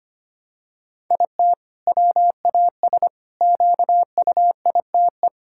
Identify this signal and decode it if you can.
Morse code, heard as a keyed tone, about 25 wpm, 700 Hz. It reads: IT WAS QUITE
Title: Sentences from Top 1000 Words - Review Part 5 - 25wpm